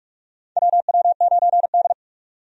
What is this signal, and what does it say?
Morse code, heard as a keyed tone, about 46 wpm, 700 Hz. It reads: WW9D
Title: USA Call Signs - 4 Characters - 45wpm